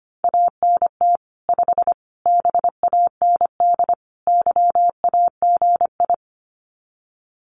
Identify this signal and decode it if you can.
Morse code, heard as a keyed tone, about 25 wpm, 700 Hz. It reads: ANT 5 BAND YAGI
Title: Common QSO Phrases 25wpm